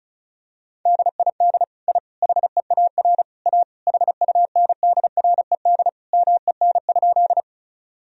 Morse code, heard as a keyed tone, about 36 wpm, 700 Hz. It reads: DID I HEAR A HUNDRED MEN?
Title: Sentences from Top 1000 Words - Review Part 2 - 35wpm